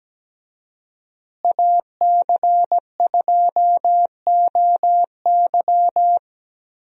Morse code, heard as a keyed tone, about 17 wpm, 700 Hz. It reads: AC2OY
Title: USA Call Signs - 5 Characters - Repeated 2x - 17wpm